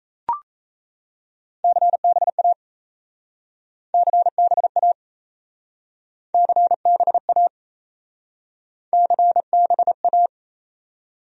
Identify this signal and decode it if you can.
Morse code, heard as a keyed tone, about 37 wpm, 700 Hz. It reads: CBA CBA CBA CBA